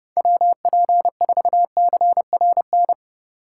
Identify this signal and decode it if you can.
Morse code, heard as a keyed tone, about 30 wpm, 700 Hz. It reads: WP4CRN
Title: USA Call Signs - 6 Characters - 4x Character Spacing - 30wpm